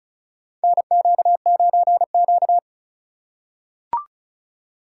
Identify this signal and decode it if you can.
Morse code, heard as a keyed tone, about 36 wpm, 700 Hz. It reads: NQ9Q
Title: USA Call Signs - 4 Characters - 3x Character Spacing - 35wpm